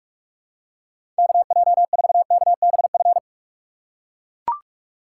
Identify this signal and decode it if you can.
Morse code, heard as a keyed tone, about 44 wpm, 700 Hz. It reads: KJ4KBF
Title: USA Call Signs - 6 Characters - 6x Character Spacing - 45wpm